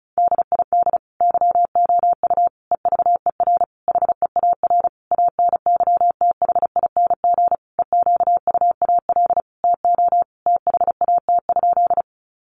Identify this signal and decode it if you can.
Morse code, heard as a keyed tone, about 34 wpm, 700 Hz. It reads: DID YOU EVER HEAR ANYTHING EQUAL TO THAT?
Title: Sentences from Top 1400 Words 35wpm